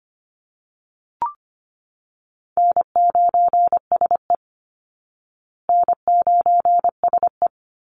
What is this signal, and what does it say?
Morse code, heard as a keyed tone, about 25 wpm, 700 Hz. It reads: N9SE N9SE
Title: Easy Call Signs Repeated 2x - 25wpm